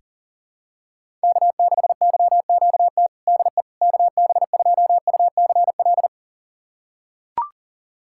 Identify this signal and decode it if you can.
Morse code, heard as a keyed tone, about 42 wpm, 700 Hz. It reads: K6YQT DE KB2UCL
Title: USA Callsign DE Callsign 40wpm